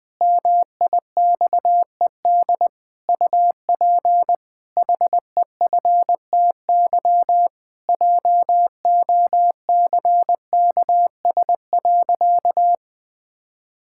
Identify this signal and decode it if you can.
Morse code, heard as a keyed tone, about 20 wpm, 700 Hz. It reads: MIXED UP HEFTY JOCKS.
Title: Pangram Challenge v2 - 20wpm